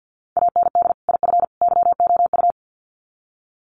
Morse code, heard as a keyed tone, about 50 wpm, 700 Hz. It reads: AND IF YOU
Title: Sets of 3 Words 50wpm